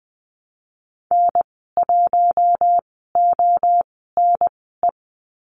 Morse code, heard as a keyed tone, about 20 wpm, spaced as for 14 wpm, 700 Hz. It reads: N1ONE
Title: 042 Lesson 8g - Number 1 - Call Signs 20wpm